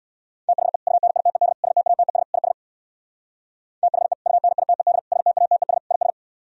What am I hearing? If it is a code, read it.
Morse code, heard as a keyed tone, about 64 wpm, 700 Hz. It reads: THE UNITED STATES IS THE UNITED STATES IS